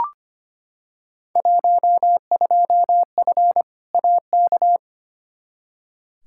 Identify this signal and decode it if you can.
Morse code, heard as a keyed tone, about 25 wpm, 700 Hz. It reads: 12F AK
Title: Field Day Exchanges Encore - 25wpm